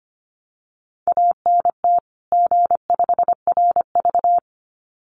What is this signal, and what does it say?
Morse code, heard as a keyed tone, about 25 wpm, 700 Hz. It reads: ANT G5RV
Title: Common QSO Phrases 25wpm